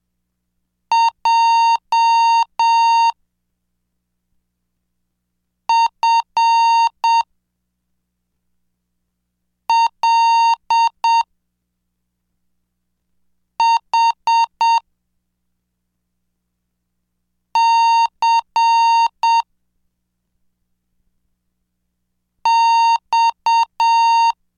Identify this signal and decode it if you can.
Morse code, heard as a keyed tone, about 7 wpm, 920 Hz. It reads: J F L H C X
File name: cursos / CDestudo / 16 Track 16